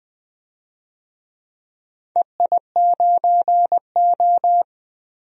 Morse code, heard as a keyed tone, about 20 wpm, 700 Hz. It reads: EI9O
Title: DXCC-100 Call Signs - 4 Characters - Repeated 2x - 20wpm